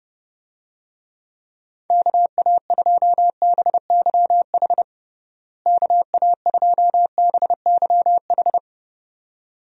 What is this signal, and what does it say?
Morse code, heard as a keyed tone, about 30 wpm, 700 Hz. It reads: KA2BYH KA2BYH
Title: USA Call Signs - 6 Characters - Repeated 2x - 30wpm